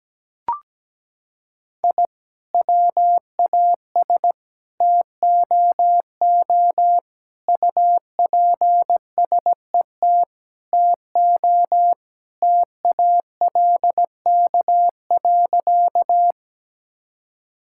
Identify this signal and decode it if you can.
Morse code, heard as a keyed tone, about 17 wpm, 700 Hz. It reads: I WAS TOO UPSET TO TALK.
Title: Sentences from Top 1000 Words - Review Part 5 - 17wpm